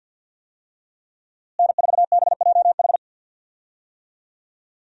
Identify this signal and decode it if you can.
Morse code, heard as a keyed tone, about 50 wpm, 700 Hz. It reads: N4BJH
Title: USA Call Signs - 5 Characters - Repeated 3x - 50wpm